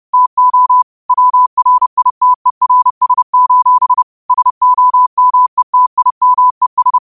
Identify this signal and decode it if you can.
Morse code, heard as a keyed tone, about 30 wpm, 1000 Hz. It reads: TO WRITERS: SOMETIMES
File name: qotdfiles / 30wpm